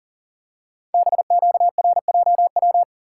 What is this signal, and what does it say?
Morse code, heard as a keyed tone, about 39 wpm, 690 Hz. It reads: BQRJW
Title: Code Groups - 5 Letters - 40wpm